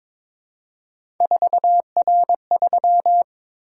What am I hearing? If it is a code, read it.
Morse code, heard as a keyed tone, about 22 wpm, 700 Hz. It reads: 4R3
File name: practice / 22wpm / Three Letter-Number Combinations - 22wpm